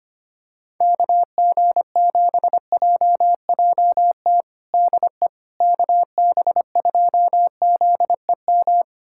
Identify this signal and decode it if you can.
Morse code, heard as a keyed tone, about 25 wpm, 700 Hz. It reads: KG7JJT DE KB2ZEM
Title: USA Callsign DE Callsign - 6 Characters - 25wpm